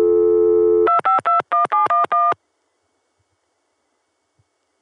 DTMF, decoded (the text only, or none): 2521711